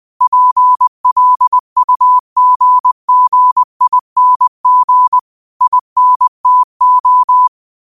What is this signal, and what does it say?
Morse code, heard as a keyed tone, about 20 wpm, 1000 Hz. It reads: PLUGGING INTO
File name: qotdfiles / 20wpm